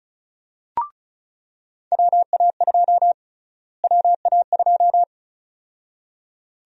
Morse code, heard as a keyed tone, about 36 wpm, 700 Hz. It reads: WA2 WA2
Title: USA Call Sign Prefixes - Repeated 2x - 35wpm